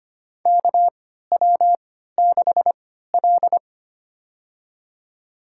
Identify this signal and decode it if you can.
Morse code, heard as a keyed tone, about 25 wpm, spaced as for 14 wpm, 700 Hz. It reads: KW6L
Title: USA Call Signs - 3x Character Spacing - 25wpm